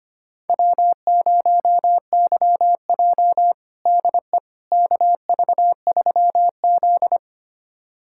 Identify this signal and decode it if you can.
Morse code, heard as a keyed tone, about 25 wpm, 700 Hz. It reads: W0YJ DE KV3Z
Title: USA Callsign DE Callsign - 4 Characters - 25wpm